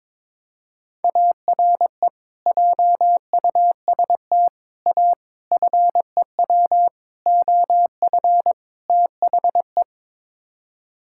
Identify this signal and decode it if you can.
Morse code, heard as a keyed tone, about 22 wpm, 700 Hz. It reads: ARE JUST A FEW OF THE
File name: practice / 22wpm / Sets of 6 Words - 22wpm